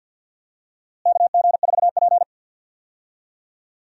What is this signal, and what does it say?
Morse code, heard as a keyed tone, about 47 wpm, 700 Hz. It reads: KG4P
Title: USA Call Signs - 4 Characters - Repeated 3x - 50wpm